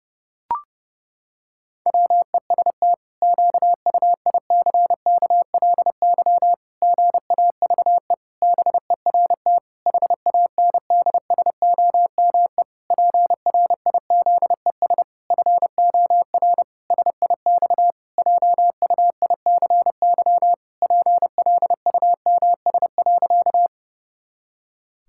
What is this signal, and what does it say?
Morse code, heard as a keyed tone, about 30 wpm, 700 Hz. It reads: WEST QUICKLY GAVE BERT HANDSOME PRIZES FOR SIX JUICY PLUMS.
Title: Pangram Challenge v2 - 30wpm